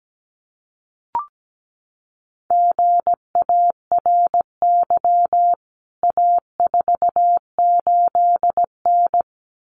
Morse code, heard as a keyed tone, about 17 wpm, 700 Hz. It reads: GARY A48N
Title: CWT Exchange 17wpm